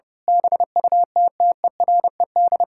Morse code, heard as a keyed tone, about 29 wpm, 700 Hz. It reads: BUTTERED
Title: Pangram Challenge v2 - 30wpm